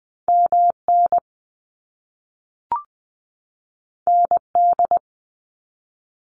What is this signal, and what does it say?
Morse code, heard as a keyed tone, about 20 wpm, 700 Hz. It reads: MN ND
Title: US States 20wpm